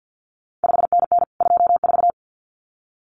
Morse code, heard as a keyed tone, about 50 wpm, 700 Hz. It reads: 5NN 24
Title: CQ WW DX Contest Exchange - 50wpm